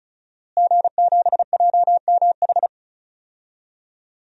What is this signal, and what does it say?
Morse code, heard as a keyed tone, about 36 wpm, 700 Hz. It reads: G7JMH
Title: DXCC-100 Call Signs - 5 Characters - Repeated 3x - 35wpm